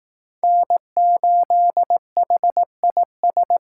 Morse code, heard as a keyed tone, about 18 wpm, 700 Hz. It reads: N8HIS
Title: Speed Racing - USA 1x3 Call Signs 15wpm